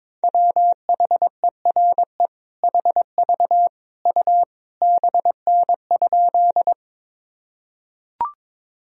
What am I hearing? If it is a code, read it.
Morse code, heard as a keyed tone, about 21 wpm, 700 Hz. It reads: WHERE HV U BN?
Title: Sentences from Top 200 Words and Abbreviations 22wpm